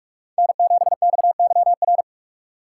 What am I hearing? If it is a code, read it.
Morse code, heard as a keyed tone, about 44 wpm, 700 Hz. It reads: N7XYR